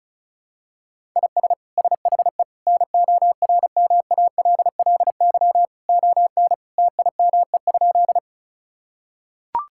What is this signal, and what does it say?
Morse code, heard as a keyed tone, about 36 wpm, 700 Hz. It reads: IS SHE NORMALLY ON TIME?
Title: Sentences from Top 1000 Words - Review Part 6 - 35wpm